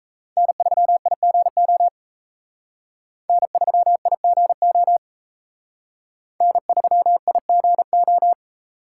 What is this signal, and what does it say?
Morse code, heard as a keyed tone, about 34 wpm, 700 Hz. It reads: N3IGO N3IGO N3IGO
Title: Speed Racing - USA 1x3 Call Signs 28wpm